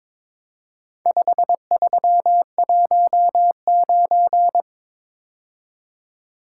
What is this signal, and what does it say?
Morse code, heard as a keyed tone, about 22 wpm, 700 Hz. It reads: 5319
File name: practice / 22wpm / Four Letter-Number Combinations - 22wpm